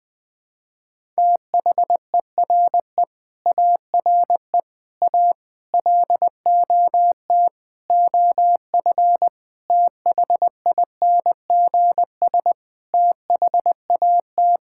Morse code, heard as a keyed tone, about 20 wpm, 700 Hz. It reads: THERE ARE A LOT OF THINGS THAT
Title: Sentences with 5-Word Phrases - Spoken Beginning - 20wpm